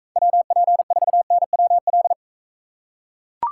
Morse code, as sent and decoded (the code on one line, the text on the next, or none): .-- .--. ....- -. .-- .-..
WP4NWL